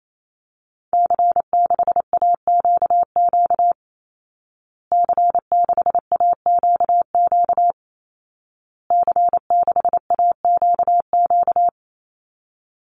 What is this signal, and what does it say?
Morse code, heard as a keyed tone, about 28 wpm, 700 Hz. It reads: C6AQQ C6AQQ C6AQQ
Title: Call Signs Repeated 2x - 28wpm